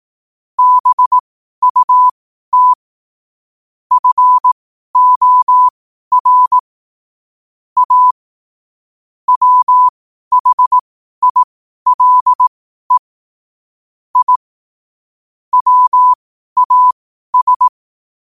Morse code, heard as a keyed tone, about 18 wpm, spaced as for 12 wpm, 1000 Hz. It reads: BUT FOR A WHILE I WAS
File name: qotdfiles / 12wpm